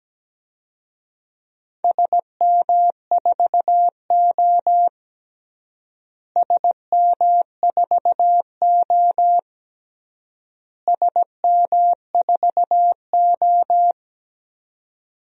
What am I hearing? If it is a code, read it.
Morse code, heard as a keyed tone, about 17 wpm, 700 Hz. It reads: SM4O SM4O SM4O
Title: DXCC-100 Call Signs - 4 Characters - Repeated 3x - 17wpm